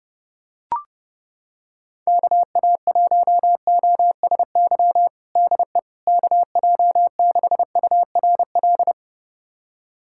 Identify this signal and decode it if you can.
Morse code, heard as a keyed tone, about 32 wpm, 690 Hz. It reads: KA1OSY DE KJ6URL
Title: USA Callsign DE Callsign - 6 Characters - 30wpm